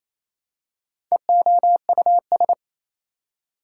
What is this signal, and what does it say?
Morse code, heard as a keyed tone, about 28 wpm, 700 Hz. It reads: EOUS
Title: Prefixes and Suffixes 28wpm